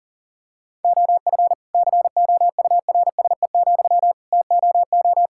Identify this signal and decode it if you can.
Morse code, heard as a keyed tone, about 40 wpm, 700 Hz. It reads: OF COURSE, TOO
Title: Sentences from Top 1000 Words - Review Part 5 - 40wpm